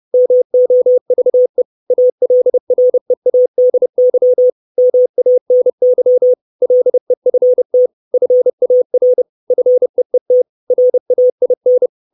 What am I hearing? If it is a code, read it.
Morse code, heard as a keyed tone, about 30 wpm, 500 Hz. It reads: MOVE ALREADY MANY LEFT FAR FEET RAIN
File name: 632 words / 632 Words 30WPM